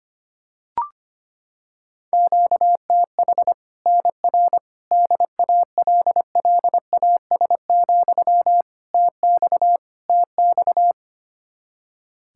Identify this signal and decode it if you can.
Morse code, heard as a keyed tone, about 25 wpm, 700 Hz. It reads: QTH NR DALLAS, TX TX
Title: Practice QSO - Exchange 1 - Beginning - 25wpm